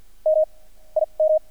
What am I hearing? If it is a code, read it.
Morse code, heard as a keyed tone, about 50 wpm, 610 Hz. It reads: M I M
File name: archived / CRSnet